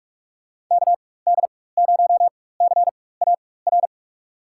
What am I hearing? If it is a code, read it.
Morse code, heard as a keyed tone, about 46 wpm, spaced as for 21 wpm, 700 Hz. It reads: KD0CAR
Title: USA Call Signs - 4x Character Spacing - 45wpm